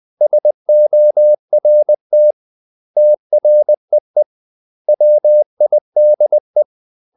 Morse code, heard as a keyed tone, about 20 wpm, 600 Hz. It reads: SORT TREE WIDE
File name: CW / Practice / 500MostCommonEnglishWords20